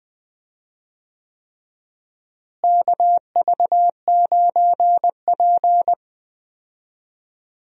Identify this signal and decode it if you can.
Morse code, heard as a keyed tone, about 20 wpm, 700 Hz. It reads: KV9P
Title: USA Call Signs - 4 Characters - Repeated 3x - 20wpm